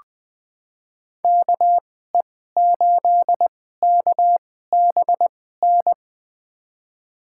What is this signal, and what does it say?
Morse code, heard as a keyed tone, about 20 wpm, spaced as for 14 wpm, 700 Hz. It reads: KE8KBN
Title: USA Call Signs - 6 Characters - 2x Character Spacing - 20wpm